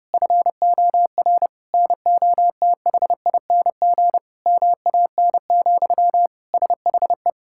Morse code, heard as a keyed tone, about 28 wpm, 700 Hz. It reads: FOR NOTHING MAN, SHE
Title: Sentences from Top 1000 Words - Review Part 6 - 30wpm